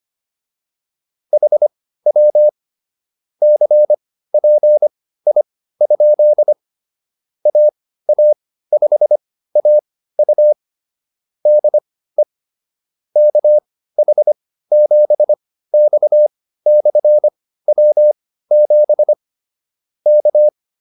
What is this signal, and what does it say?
Morse code, heard as a keyed tone, about 25 wpm, spaced as for 15 wpm, 600 Hz. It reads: HW CPI? AA5AU DE KH7X/W7 K
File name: QSO104 15